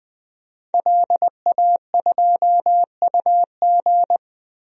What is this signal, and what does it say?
Morse code, heard as a keyed tone, about 21 wpm, 700 Hz. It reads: LA2UG
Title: DXCC-100 Call Signs - 5 Characters - Repeated 2x - 20wpm